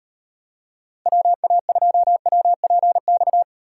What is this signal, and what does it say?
Morse code, heard as a keyed tone, about 38 wpm, 700 Hz. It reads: WA2WPX